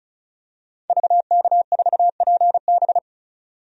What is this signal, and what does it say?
Morse code, heard as a keyed tone, about 36 wpm, 700 Hz. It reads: VK4PB
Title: DXCC-100 Call Signs - 5 Characters - 35wpm